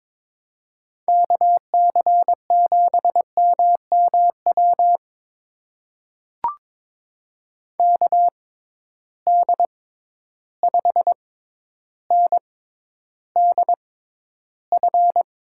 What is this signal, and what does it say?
Morse code, heard as a keyed tone, about 22 wpm, spaced as for 11 wpm, 700 Hz. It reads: KC7MMW KD5NDF
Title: USA Call Signs - 6 Characters - 6x Character Spacing - 22wpm